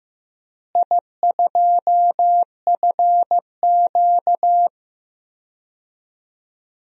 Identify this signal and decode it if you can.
Morse code, heard as a keyed tone, about 16 wpm, 700 Hz. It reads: I2FQ
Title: DXCC-100 Call Signs - 4 Characters - Repeated 3x - 15wpm